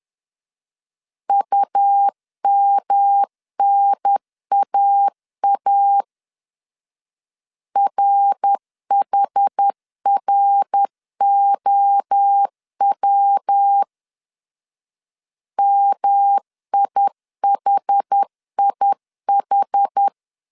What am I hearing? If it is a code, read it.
Morse code, heard as a keyed tone, about 10 wpm, 800 Hz. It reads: UMNAA RHROW MIHIH
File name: CW / DET / 1 3 1500C S C 10wpm 3335